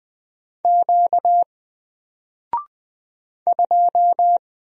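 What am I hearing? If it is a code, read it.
Morse code, heard as a keyed tone, about 20 wpm, 700 Hz. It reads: Q 2
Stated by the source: Single letter and number Morse drill at 20 wpm, from N4LCD.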